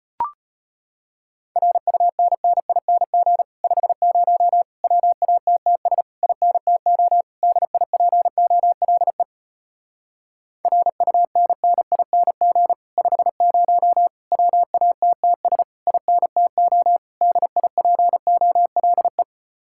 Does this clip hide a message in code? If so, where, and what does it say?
Yes: Morse code from 1.6 to 19.2 s, heard as a keyed tone, about 33 wpm, 700 Hz. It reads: RUNNING 50 WATTS INTO DIPOLE RUNNING 50 WATTS INTO DIPOLE